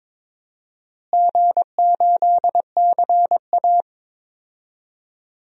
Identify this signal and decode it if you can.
Morse code, heard as a keyed tone, about 22 wpm, 700 Hz. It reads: G8CA
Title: DXCC-100 Call Signs - 4 Characters - Repeated 2x - 22wpm